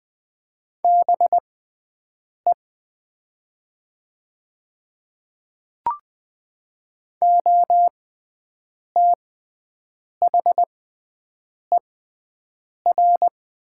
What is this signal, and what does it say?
Morse code, heard as a keyed tone, about 20 wpm, spaced as for 7 wpm, 700 Hz. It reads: BE OTHER
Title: 6x Character Spacing - Top 100 Words - 20wpm